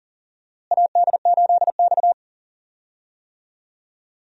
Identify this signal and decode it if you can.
Morse code, heard as a keyed tone, about 40 wpm, 700 Hz. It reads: AD8X
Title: USA Call Signs - 4 Characters - Repeated 3x - 40wpm